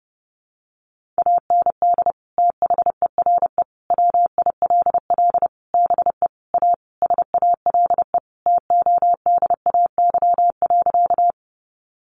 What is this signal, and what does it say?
Morse code, heard as a keyed tone, about 30 wpm, 700 Hz. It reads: AND THERE WILL BE A SALE TODAY.
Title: Sentences from Top 1400 Words 30wpm